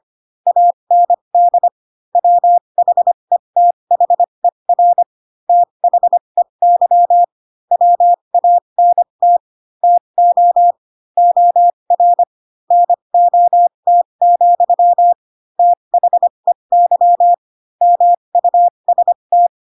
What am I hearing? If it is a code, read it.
Morse code, heard as a keyed tone, about 25 wpm, spaced as for 22 wpm, 700 Hz. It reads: AND WHETHER THEY WANT TO OR NOT, THEY MUST
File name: Pinocchio / Pinocchio ch4 18 25 ik1hge